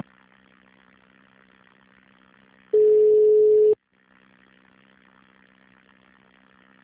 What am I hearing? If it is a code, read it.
Ringback tone (Europe, 425 Hz).